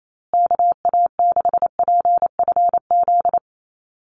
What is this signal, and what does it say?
Morse code, heard as a keyed tone, about 28 wpm, 700 Hz. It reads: KA6PFZ
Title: USA Call Signs Repeated 2x 28wpm